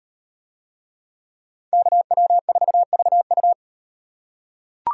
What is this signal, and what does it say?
Morse code, heard as a keyed tone, about 35 wpm, 700 Hz. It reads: KW4VU